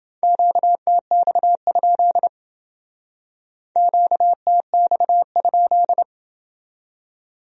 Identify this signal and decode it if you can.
Morse code, heard as a keyed tone, about 28 wpm, 700 Hz. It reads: QTX? QTX?